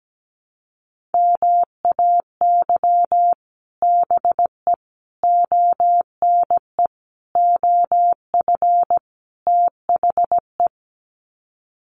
Morse code, heard as a keyed tone, about 17 wpm, 700 Hz. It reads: MAY BE ONE OF THE
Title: Sets of 5 Words Encore2 17wpm